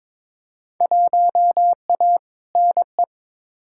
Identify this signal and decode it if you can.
Morse code, heard as a keyed tone, about 22 wpm, 700 Hz. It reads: 1A NE
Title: Field Day Exchanges Encore - 22wpm